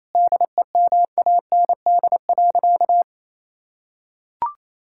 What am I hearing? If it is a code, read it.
Morse code, heard as a keyed tone, about 27 wpm, 700 Hz. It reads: DEMAND.
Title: Sentences from Top 1000 Words - Review Part 5 - 28wpm